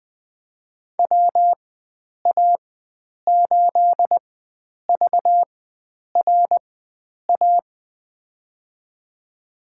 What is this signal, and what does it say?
Morse code, heard as a keyed tone, about 20 wpm, spaced as for 9 wpm, 700 Hz. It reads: WA8VRA
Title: USA Call Signs - 4x Character Spacing - 20wpm